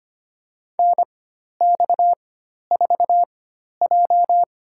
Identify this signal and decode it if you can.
Morse code, heard as a keyed tone, about 25 wpm, spaced as for 12 wpm, 700 Hz. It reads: NX4J
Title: USA Call Signs - 4x Character Spacing - 25wpm